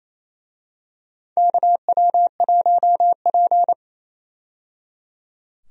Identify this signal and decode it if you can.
Morse code, heard as a keyed tone, about 28 wpm, 690 Hz. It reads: KW1P
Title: USA Call Signs - 4 Characters - 28wpm